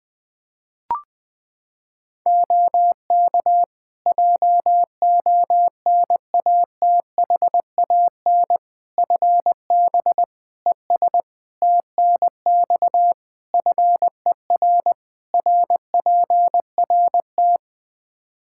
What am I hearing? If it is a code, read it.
Morse code, heard as a keyed tone, about 20 wpm, 700 Hz. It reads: OK JONATHAN FB ES TNX FER RPRT
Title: Practice QSO - Exchange 2 - Info - 20wpm